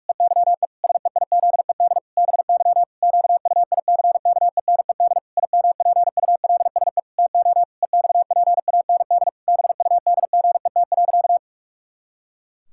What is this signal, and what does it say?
Morse code, heard as a keyed tone, about 41 wpm, 700 Hz. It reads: EYE SEIZED BY QUICKENED IMPULSE TO EXPAND BUDGET.
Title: Pangram Challenge v2 - 45wpm